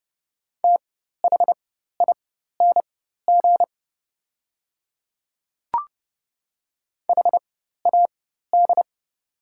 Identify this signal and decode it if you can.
Morse code, heard as a keyed tone, about 30 wpm, spaced as for 14 wpm, 690 Hz. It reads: THING HAD
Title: 4x Character Spacing - Top 100 Words - 30wpm